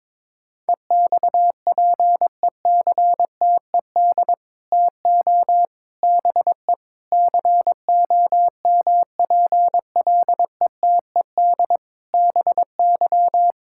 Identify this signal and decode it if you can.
Morse code, heard as a keyed tone, about 22 wpm, 700 Hz. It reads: EXPECTED TO BE COMPLETED BY
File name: practice / 22wpm / Sets of 8 Words - 22wpm